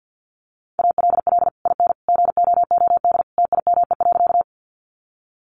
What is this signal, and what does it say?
Morse code, heard as a keyed tone, about 48 wpm, 700 Hz. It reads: ALL IN GOOD TIME.
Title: Sentences from Top 200 Words 50wpm